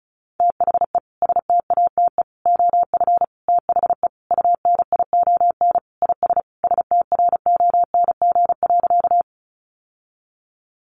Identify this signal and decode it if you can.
Morse code, heard as a keyed tone, about 34 wpm, 700 Hz. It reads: THE STATE OF THE UNION IS STRONG.
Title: Sentences from Top 1400 Words 35wpm